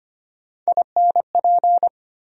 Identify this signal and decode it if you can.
Morse code, heard as a keyed tone, about 25 wpm, 700 Hz. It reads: INP
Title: USA Call Sign Postfixes - 3 Letters 25wpm